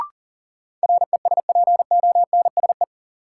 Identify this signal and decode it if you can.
Morse code, heard as a keyed tone, about 40 wpm, 700 Hz. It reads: RESPONSE
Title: Sentences from Top 1400 Words - New Word Drill Alt - 40wpm